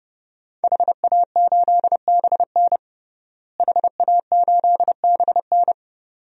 Morse code, heard as a keyed tone, about 30 wpm, 700 Hz. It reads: HA8BN HA8BN
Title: DXCC-100 Call Signs - 5 Characters - Repeated 2x - 30wpm